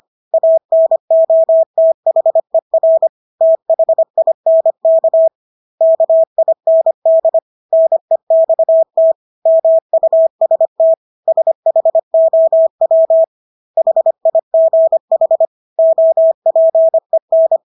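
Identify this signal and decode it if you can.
Morse code, heard as a keyed tone, about 25 wpm, 650 Hz. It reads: ANOTHER THINK KIND NEXT MUST SHOW HIGH OPEN
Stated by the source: Morse practice of English words at 25 wpm from CW med Gnister.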